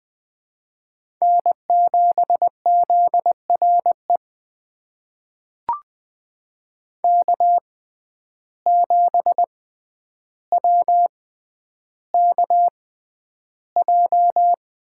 Morse code, heard as a keyed tone, about 20 wpm, spaced as for 10 wpm, 700 Hz. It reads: N7ZRE K7WKJ
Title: USA Call Signs - 5 Characters - 6x Character Spacing - 20wpm